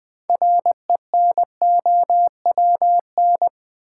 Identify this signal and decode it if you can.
Morse code, heard as a keyed tone, about 20 wpm, 700 Hz. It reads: RENOWN
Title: Build A Word - 3 Letters At A Time - 8-Letter Words - 20wpm